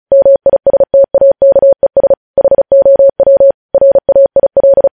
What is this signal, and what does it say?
Morse code, heard as a keyed tone, about 36 wpm, 550 Hz. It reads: MISTAKES HOW RAIL